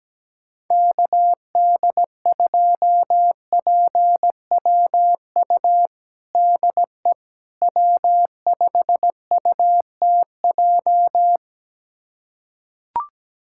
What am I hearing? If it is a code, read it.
Morse code, heard as a keyed tone, about 17 wpm, 700 Hz. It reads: KD2PWU DE W5UTJ
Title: USA Callsign DE Callsign 17wpm